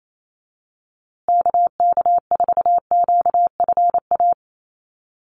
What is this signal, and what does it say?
Morse code, heard as a keyed tone, about 28 wpm, 700 Hz. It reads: KK4QFA
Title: USA Call Signs Repeated 3x 28wpm